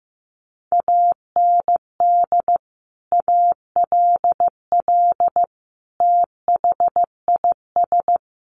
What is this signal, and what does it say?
Morse code, heard as a keyed tone, about 15 wpm, 700 Hz. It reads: AND ALL THIS
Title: Sentences from Top 1200 Words 15wpm